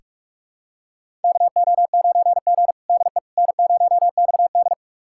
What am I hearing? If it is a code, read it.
Morse code, heard as a keyed tone, about 46 wpm, 700 Hz. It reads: KO9G DE N0XD
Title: USA Callsign DE Callsign - 4 Characters - 45wpm